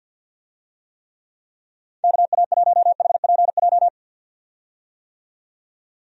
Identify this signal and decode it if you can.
Morse code, heard as a keyed tone, about 51 wpm, 700 Hz. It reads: KA1HPJ
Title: USA Call Signs - 6 Characters - 50wpm